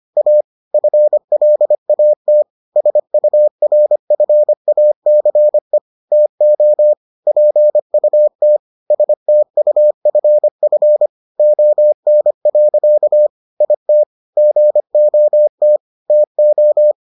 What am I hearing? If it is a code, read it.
Morse code, heard as a keyed tone, about 25 wpm, 600 Hz. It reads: A FLAT SURFACE TO PUT STUFF ON. IT GOT TO